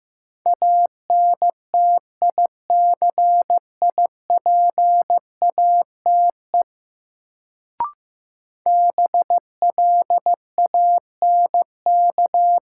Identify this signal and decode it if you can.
Morse code, heard as a keyed tone, about 15 wpm, 700 Hz. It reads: ANTICIPATE BLANK
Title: Sentences from Top 1500 Words - New Word Drill - 15wpm